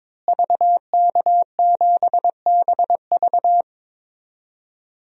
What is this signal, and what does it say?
Morse code, heard as a keyed tone, about 22 wpm, 700 Hz. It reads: VK7BV
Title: DXCC-100 Call Signs - 5 Characters - Repeated 2x - 22wpm